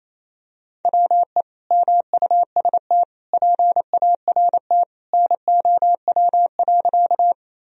Morse code, heard as a keyed tone, about 28 wpm, 700 Hz. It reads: WE MUST PART NOW.
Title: Sentences from Top 1000 Words - Review Part 1 - 28wpm